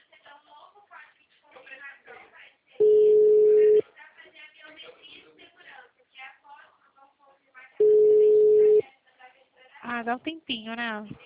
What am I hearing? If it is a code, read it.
Ringback tone (Europe, 425 Hz), with speech.